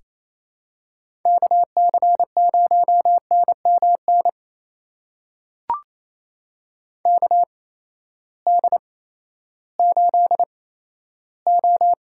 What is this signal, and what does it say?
Morse code, heard as a keyed tone, about 28 wpm, spaced as for 14 wpm, 700 Hz. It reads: KC0NMN KD8O
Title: USA Call Signs - 6 Characters - 8x Character Spacing - 28wpm